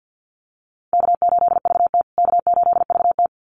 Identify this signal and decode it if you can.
Morse code, heard as a keyed tone, about 48 wpm, 700 Hz. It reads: K8VT K8VT
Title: USA Call Signs Repeated 2x 50wpm